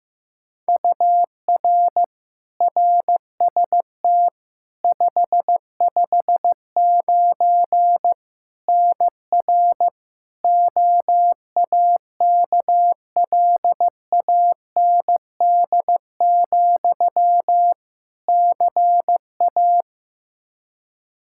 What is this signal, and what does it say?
Morse code, heard as a keyed tone, about 15 wpm, 700 Hz. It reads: UR RST 559 NR OAKLAND, CA
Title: K1USN SST QSO Practice 15wpm